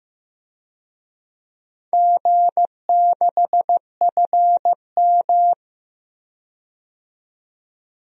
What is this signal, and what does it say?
Morse code, heard as a keyed tone, about 16 wpm, 700 Hz. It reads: G6FM
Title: DXCC-100 Call Signs - 4 Characters - 15wpm